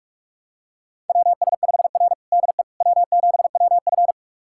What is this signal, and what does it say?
Morse code, heard as a keyed tone, about 44 wpm, 690 Hz. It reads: WS5R DE W7WF
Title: USA Callsign DE Callsign - 4 Characters - 45wpm